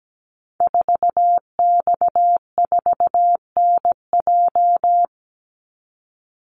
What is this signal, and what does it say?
Morse code, heard as a keyed tone, about 17 wpm, 700 Hz. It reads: 4X4NJ
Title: Call Signs - 17wpm